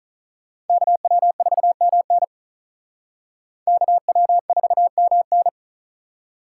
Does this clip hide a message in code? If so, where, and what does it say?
Yes: Morse code from 0.7 to 5.5 s, heard as a keyed tone, about 45 wpm, 700 Hz. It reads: KW4MN KW4MN